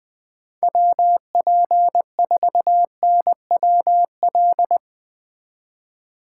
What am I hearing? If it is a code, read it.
Morse code, heard as a keyed tone, about 20 wpm, 700 Hz. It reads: WP4NWL